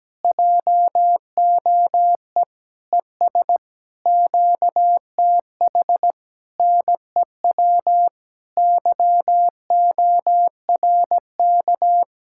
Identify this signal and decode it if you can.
Morse code, heard as a keyed tone, about 17 wpm, 700 Hz. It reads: JOE ES QTH NEW YORK
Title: Speed-Racing Common QSO Phrases - 17wpm